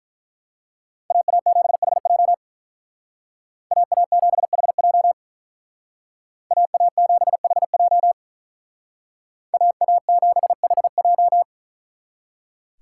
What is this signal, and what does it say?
Morse code, heard as a keyed tone, about 42 wpm, 700 Hz. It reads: AA7HJ AA7HJ AA7HJ AA7HJ
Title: Speed Racing - USA 2x2 Call Signs 35wpm